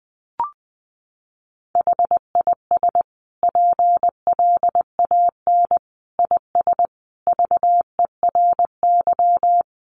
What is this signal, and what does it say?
Morse code, heard as a keyed tone, about 19 wpm, 700 Hz. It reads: HIS PLAN IS VERY
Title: Sentences from Top 500 Words 20wpm